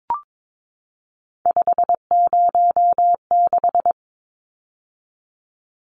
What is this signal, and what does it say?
Morse code, heard as a keyed tone, about 22 wpm, 700 Hz. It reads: 506
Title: Numbers - 3-Digit 22wpm